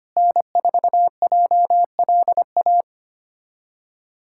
Morse code, heard as a keyed tone, about 25 wpm, 700 Hz. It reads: N4JLA
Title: USA Callsign DE Callsign - 5 Characters - 25wpm